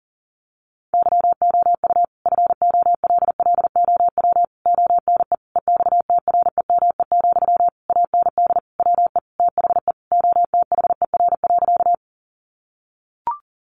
Morse code, heard as a keyed tone, about 40 wpm, 700 Hz. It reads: YOU FOLLOW ONE EXTREME, AND WE THE OTHER.
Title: Sentences from Top 1300 Words 40wpm